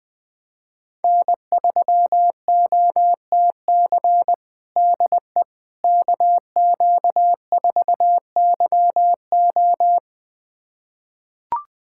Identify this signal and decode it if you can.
Morse code, heard as a keyed tone, about 21 wpm, 700 Hz. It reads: N3OTC DE KQ4YO
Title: USA Callsign DE Callsign - 5 Characters - 20wpm